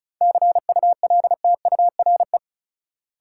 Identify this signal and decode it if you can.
Morse code, heard as a keyed tone, about 34 wpm, 700 Hz. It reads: CULTURE
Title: Sentences from Top 1000 Words - Review Part 6 - New Word Drill - 35wpm